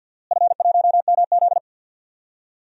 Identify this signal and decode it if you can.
Morse code, heard as a keyed tone, about 47 wpm, 700 Hz. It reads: F1MZ
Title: DXCC-100 Call Signs - 4 Characters - Repeated 3x - 50wpm